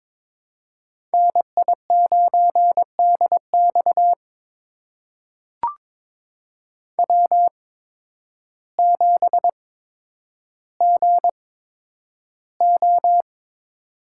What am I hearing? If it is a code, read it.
Morse code, heard as a keyed tone, about 22 wpm, spaced as for 10 wpm, 700 Hz. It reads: NI9DX W7GO
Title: USA Call Signs - 8x Character Spacing - 22wpm